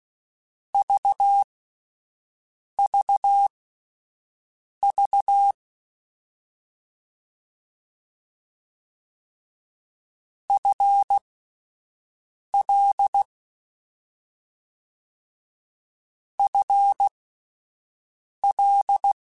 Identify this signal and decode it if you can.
Morse code, heard as a keyed tone, about 16 wpm, spaced as for 5 wpm, 800 Hz. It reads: VVV FL FL
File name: Combinations / Combinations-mp3s / Combinations@5